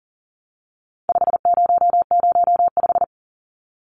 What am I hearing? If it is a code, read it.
Morse code, heard as a keyed tone, about 40 wpm, 700 Hz. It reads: 5005
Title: Numbers - 4-Digit 40wpm